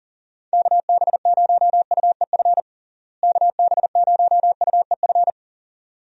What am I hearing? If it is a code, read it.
Morse code, heard as a keyed tone, about 37 wpm, 700 Hz. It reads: KB0UEF KB0UEF
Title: USA Call Signs - 6 Characters - Repeated 2x - 40wpm